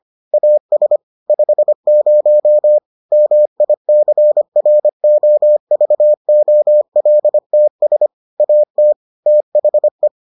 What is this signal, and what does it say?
Morse code, heard as a keyed tone, about 25 wpm, 600 Hz. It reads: AS 50 MICROVOLTS AT THE